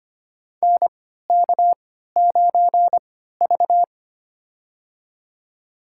Morse code, heard as a keyed tone, about 25 wpm, spaced as for 14 wpm, 700 Hz. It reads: NK9V
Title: USA Call Signs - 4 Characters - 3x Character Spacing - 25wpm